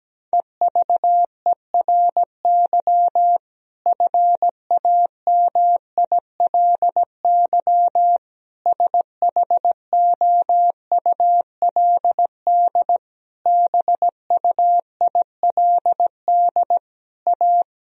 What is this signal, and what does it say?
Morse code, heard as a keyed tone, about 17 wpm, 700 Hz. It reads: EVERY FAMILY SHOULD BUILD A
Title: Sentences from Top 1000 Words - Review Part 1 - 17wpm